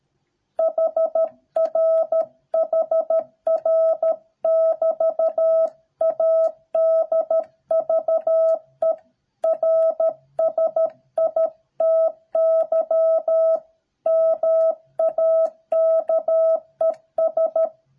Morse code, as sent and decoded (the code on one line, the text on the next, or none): .... .-. / .... .-. / -...- / .- -.. ...- . .-. ... .. - -.-- / -- .- -.- . ...
HR HR <BT> ADVERSITY MAKES